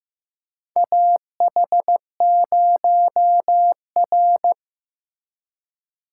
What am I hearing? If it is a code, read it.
Morse code, heard as a keyed tone, about 15 wpm, 700 Hz. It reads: AH0R
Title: USA Call Signs - 4 Characters - 3x Character Spacing - 15wpm